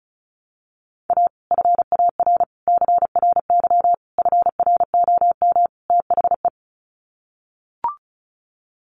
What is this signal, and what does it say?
Morse code, heard as a keyed tone, about 34 wpm, 700 Hz. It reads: A FAR CRY FROM THE
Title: Sets of 5 Words Encore 35wpm